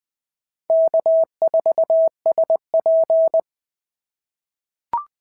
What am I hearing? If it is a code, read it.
Morse code, heard as a keyed tone, about 19 wpm, 650 Hz. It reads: K4SP
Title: POTA QSO Practice - 20wpm